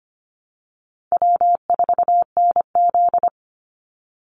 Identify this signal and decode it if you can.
Morse code, heard as a keyed tone, about 25 wpm, 700 Hz. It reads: W4NZ
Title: Call Signs Repeated 3x - 25wpm